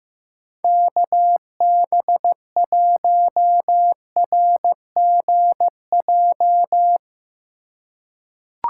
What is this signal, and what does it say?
Morse code, heard as a keyed tone, about 16 wpm, 700 Hz. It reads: KB1RGJ
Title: USA Call Signs - 6 Characters - 3x Character Spacing - 15wpm